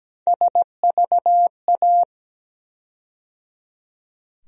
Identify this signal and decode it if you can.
Morse code, heard as a keyed tone, about 17 wpm, 700 Hz. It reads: SVA
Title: USA Call Sign Postfixes - 3 Letters 17wpm